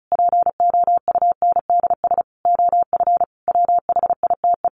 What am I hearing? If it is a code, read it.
Morse code, heard as a keyed tone, about 34 wpm, 700 Hz. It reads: POUNDS OF WHITE
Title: A Sound Of Thunder 35wpm